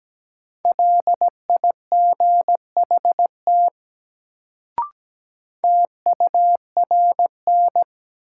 Morse code, heard as a keyed tone, about 17 wpm, 700 Hz. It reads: LIGHT TURN
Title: Sentences from Top 200 Words - New Word Drill - 17wpm